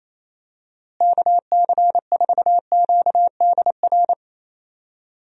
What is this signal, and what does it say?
Morse code, heard as a keyed tone, about 28 wpm, 700 Hz. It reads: KC4QDR
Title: USA Call Signs - 6 Characters - Repeated 2x - 28wpm